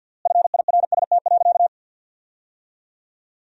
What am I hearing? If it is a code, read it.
Morse code, heard as a keyed tone, about 50 wpm, 700 Hz. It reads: FIRST.
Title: 3x Word-Spacing - Sentences from Top 400 Words - 50wpm